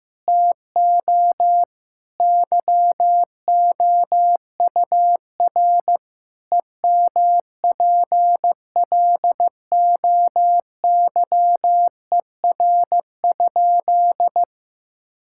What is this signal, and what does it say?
Morse code, heard as a keyed tone, about 16 wpm, 700 Hz. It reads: TO YOUR EMPLOYER?
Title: Sentences from Top 1600 Words 15wpm